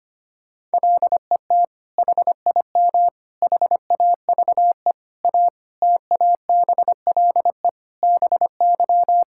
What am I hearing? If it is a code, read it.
Morse code, heard as a keyed tone, about 25 wpm, 700 Hz. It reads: LET HIM HAVE A TABLE BY
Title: Sentences from Top 1600 Words 25wpm